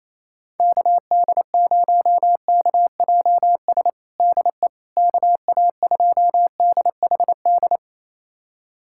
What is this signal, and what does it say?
Morse code, heard as a keyed tone, about 28 wpm, 700 Hz. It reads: KD0KJS DE KA2DHD
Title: USA Callsign DE Callsign - 6 Characters - 28wpm